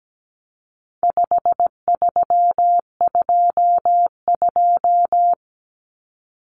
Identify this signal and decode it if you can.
Morse code, heard as a keyed tone, about 17 wpm, 700 Hz. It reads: 5322
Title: Common Cut Numbers - 4-Digit 17wpm